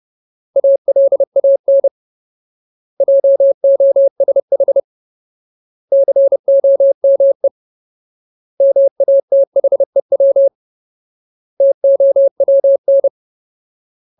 Morse code, heard as a keyed tone, about 30 wpm, 550 Hz. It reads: ALAN JOSH COME MATHEW TOWN
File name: Ultimate Word List / 30WPM Ultimate